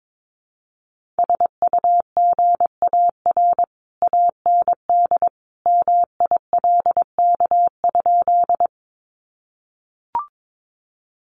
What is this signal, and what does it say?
Morse code, heard as a keyed tone, about 22 wpm, 700 Hz. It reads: SUGAR AND MILK?
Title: Sentences from Top 1500 Words 22wpm